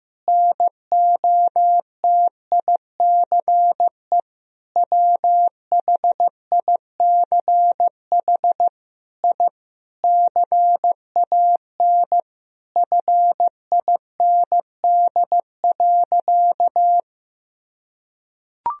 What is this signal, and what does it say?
Morse code, heard as a keyed tone, about 15 wpm, 700 Hz. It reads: NOTICE WHICH I CAN FIND.
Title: Sentences from Top 1000 Words - Review Part 2 - 15wpm